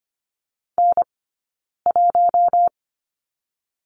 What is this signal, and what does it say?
Morse code, heard as a keyed tone, about 25 wpm, 700 Hz. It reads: N 1
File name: course / 25:10wpm / 092 Lesson 16d - Letter c - Two Characters 25wpm